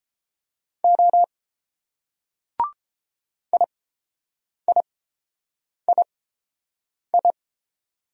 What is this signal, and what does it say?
Morse code, heard as a keyed tone, about 30 wpm, spaced as for 7 wpm, 700 Hz. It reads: O IIII